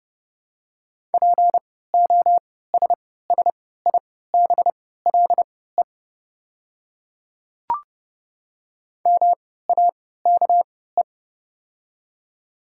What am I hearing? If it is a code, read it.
Morse code, heard as a keyed tone, about 30 wpm, spaced as for 17 wpm, 700 Hz. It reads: POSSIBLE MAKE
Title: 3x Character Spacing - Top 500 Words - 30wpm